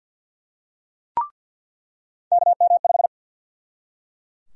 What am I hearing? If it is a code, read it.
Morse code, heard as a keyed tone, about 46 wpm, 700 Hz. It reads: KM5
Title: USA Call Sign Prefixes - 50wpm